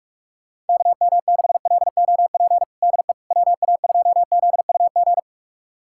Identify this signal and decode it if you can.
Morse code, heard as a keyed tone, about 46 wpm, 700 Hz. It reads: KM6LOP DE WA2ZUG
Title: USA Callsign DE Callsign - 6 Characters - 45wpm